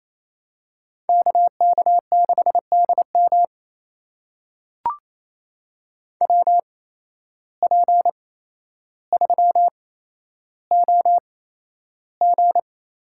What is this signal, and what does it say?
Morse code, heard as a keyed tone, about 28 wpm, spaced as for 12 wpm, 700 Hz. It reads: KK6DM WP3OG
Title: USA Call Signs - 5 Characters - 8x Character Spacing - 28wpm